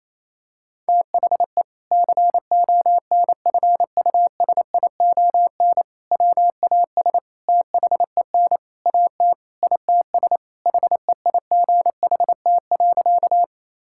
Morse code, heard as a keyed tone, about 28 wpm, 700 Hz. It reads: THE CONFUSION WAS THEN AT ITS HEIGHT.
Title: Sentences from Top 1600 Words 28wpm